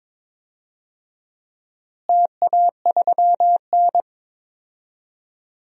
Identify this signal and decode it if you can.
Morse code, heard as a keyed tone, about 22 wpm, 700 Hz. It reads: TA3N
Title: DXCC-100 Call Signs - 4 Characters - Repeated 2x - 22wpm